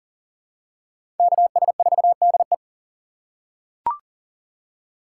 Morse code, heard as a keyed tone, about 39 wpm, 700 Hz. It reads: KS4DE
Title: USA Call Signs - 5 Characters - 2x Character Spacing - 40wpm